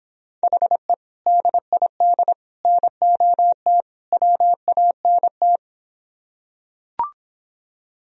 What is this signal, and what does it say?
Morse code, heard as a keyed tone, about 26 wpm, 700 Hz. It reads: HE DID NOT WANT